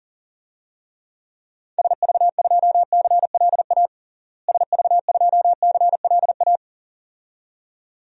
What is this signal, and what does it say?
Morse code, heard as a keyed tone, about 40 wpm, 700 Hz. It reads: SV2CLA SV2CLA
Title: DXCC-100 Call Signs - 6 Characters - Repeated 2x - 40wpm